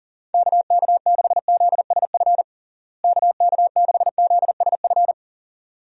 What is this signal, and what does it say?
Morse code, heard as a keyed tone, about 40 wpm, 700 Hz. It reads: KK6ZSF KK6ZSF
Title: USA Call Signs - 6 Characters - Repeated 2x - 40wpm